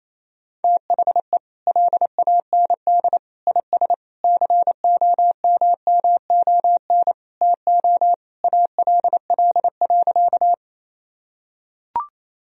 Morse code, heard as a keyed tone, about 28 wpm, 700 Hz. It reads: THE LAND IS COMMON TO ALL.
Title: Sentences from Top 1000 Words - Review Part 3 - 28wpm